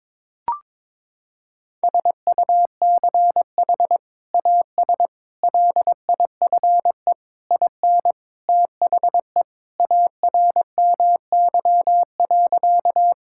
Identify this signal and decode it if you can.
Morse code, heard as a keyed tone, about 22 wpm, 700 Hz. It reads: SUCH AS LIFE IN THE ARMY.
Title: Sentences from Top 1000 Words - Review Part 5 - 22wpm